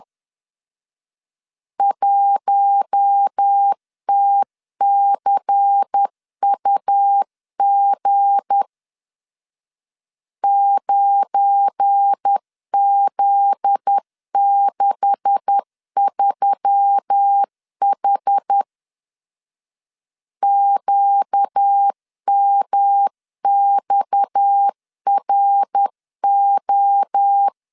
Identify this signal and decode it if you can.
Morse code, heard as a keyed tone, about 10 wpm, 800 Hz. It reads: 1TCUG 9Z63H QMXRO
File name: LEZIONI / 9 286 10wpm 2735